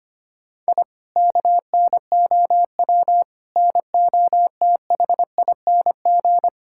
Morse code, heard as a keyed tone, about 25 wpm, 700 Hz. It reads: I KNOW NOTHING
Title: Sentences from Top 1000 Words - Review Part 6 - 25wpm